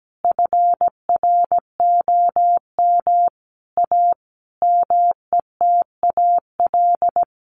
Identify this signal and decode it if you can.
Morse code, heard as a keyed tone, about 17 wpm, 700 Hz. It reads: FROM A METAL
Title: A Sound Of Thunder 17wpm